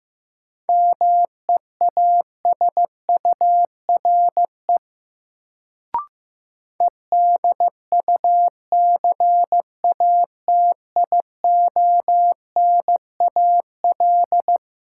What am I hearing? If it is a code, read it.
Morse code, heard as a keyed tone, about 15 wpm, 700 Hz. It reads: MEASURE EDUCATIONAL
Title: Sentences from Top 1200 Words - New Word Drill Alt - 15wpm